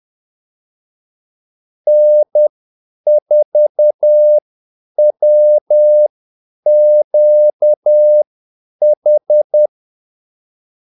Morse code, heard as a keyed tone, about 10 wpm, spaced as for 8 wpm, 600 Hz. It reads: N4WQH
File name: Callsigns-10-8wpm